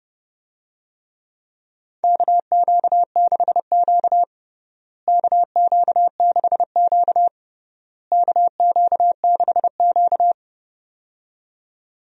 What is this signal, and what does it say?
Morse code, heard as a keyed tone, about 30 wpm, 700 Hz. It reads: KQ6Q KQ6Q KQ6Q
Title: USA Call Signs - 4 Characters - Repeated 3x - 30wpm